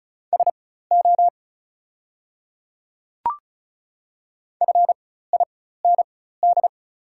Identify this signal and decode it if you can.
Morse code, heard as a keyed tone, about 36 wpm, spaced as for 16 wpm, 700 Hz. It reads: SO FIND
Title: 4x Character Spacing - Top 100 Words - 35wpm